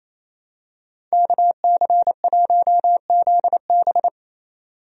Morse code, heard as a keyed tone, about 28 wpm, 700 Hz. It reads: KC1ZB
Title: USA Call Signs - 5 Characters - 28wpm